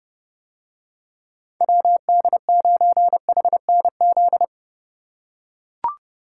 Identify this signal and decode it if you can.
Morse code, heard as a keyed tone, about 31 wpm, 700 Hz. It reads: WD9HNZ